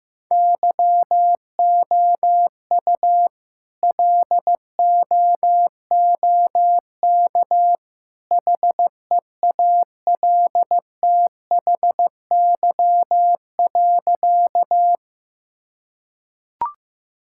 Morse code, heard as a keyed tone, about 16 wpm, 700 Hz. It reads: YOU LOOK HEALTHY.
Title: Sentences from Top 1000 Words - Review Part 5 - 15wpm